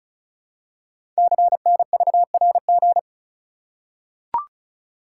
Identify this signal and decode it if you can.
Morse code, heard as a keyed tone, about 35 wpm, 690 Hz. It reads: CNVRG
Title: Code Groups - 5 Letters - 35wpm